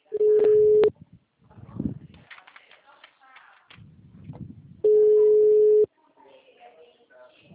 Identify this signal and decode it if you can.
Ringback tone (Europe, 425 Hz), with speech.